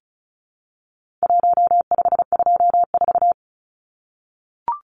Morse code, heard as a keyed tone, about 34 wpm, 700 Hz. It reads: 1524
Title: Numbers - 4-Digit 35wpm